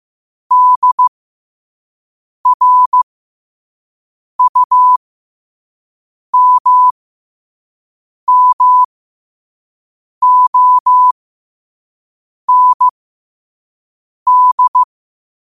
Morse code, heard as a keyed tone, about 15 wpm, spaced as for 5 wpm, 1000 Hz. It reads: DRUMMOND
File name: qotdfiles / 5wpm